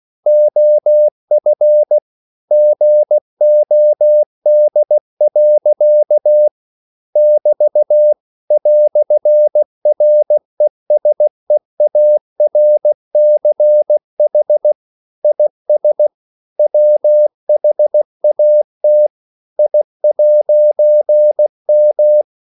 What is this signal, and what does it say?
Morse code, heard as a keyed tone, about 16 wpm, 600 Hz. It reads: OF GOD. <BT> "RESEARCH IS WHAT I'M
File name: CW / Practice / EnglishQuotations0416